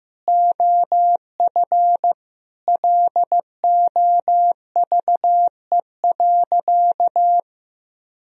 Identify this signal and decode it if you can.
Morse code, heard as a keyed tone, about 16 wpm, 700 Hz. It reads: OF LOVE.
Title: Sentences from Top 1000 Words - Review Part 2 - 15wpm